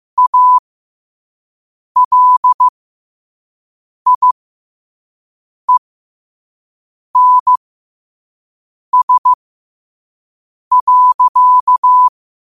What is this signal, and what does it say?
Morse code, heard as a keyed tone, about 15 wpm, spaced as for 5 wpm, 1000 Hz. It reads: ALIENS.
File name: qotdfiles / 5wpm